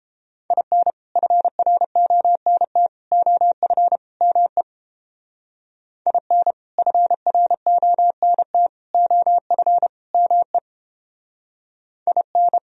Morse code, heard as a keyed tone, about 29 wpm, 700 Hz. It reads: IN FRONT OF ME IN FRONT OF ME IN